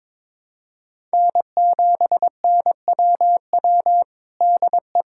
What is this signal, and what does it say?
Morse code, heard as a keyed tone, about 22 wpm, 700 Hz. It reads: N7NWW DE
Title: Practice QSO - Exchange 4 - Ending - 22wpm